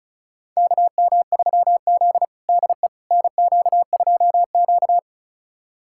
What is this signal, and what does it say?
Morse code, heard as a keyed tone, about 36 wpm, 700 Hz. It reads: KM3Z DE NQ2Q
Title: USA Callsign DE Callsign - 4 Characters - 35wpm